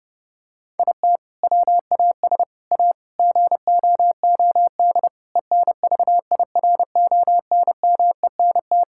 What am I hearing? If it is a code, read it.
Morse code, heard as a keyed tone, about 29 wpm, 700 Hz. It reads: IT WAS A GOOD ENVIRONMENT
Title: Sentences from Top 1000 Words - Review Part 5 - 30wpm